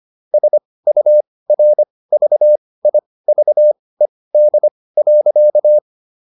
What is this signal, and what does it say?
Morse code, heard as a keyed tone, about 25 wpm, spaced as for 18 wpm, 600 Hz. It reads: SURVIVED.